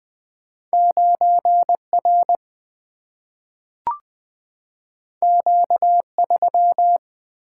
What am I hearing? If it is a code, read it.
Morse code, heard as a keyed tone, about 20 wpm, 700 Hz. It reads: 9R Q3
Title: Single-Letter Single-Number 20wpm